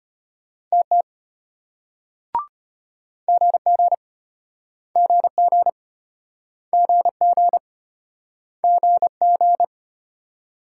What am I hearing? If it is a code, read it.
Morse code, heard as a keyed tone, about 32 wpm, 700 Hz. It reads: TT GG GG GG GG